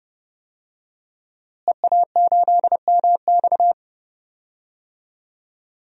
Morse code, heard as a keyed tone, about 30 wpm, 700 Hz. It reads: EA8MX
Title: DXCC-100 Call Signs - 5 Characters - Repeated 3x - 30wpm